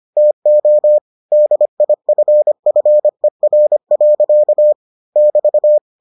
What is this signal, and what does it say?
Morse code, heard as a keyed tone, about 25 wpm, 600 Hz. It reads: TO DIFFER. <BT>